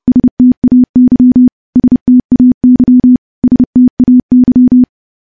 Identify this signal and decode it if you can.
Morse code, heard as a keyed tone, about 32 wpm, 250 Hz. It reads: STAY STAY STAY
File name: morse / morse-training-30WPM